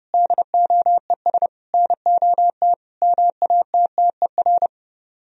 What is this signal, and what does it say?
Morse code, heard as a keyed tone, about 30 wpm, 700 Hz. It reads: DOES NOT MATTER